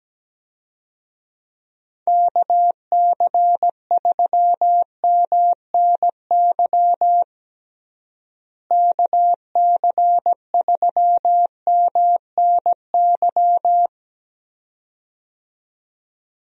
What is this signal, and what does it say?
Morse code, heard as a keyed tone, about 17 wpm, 700 Hz. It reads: KC3MNY KC3MNY
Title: USA Call Signs - 6 Characters - Repeated 2x - 17wpm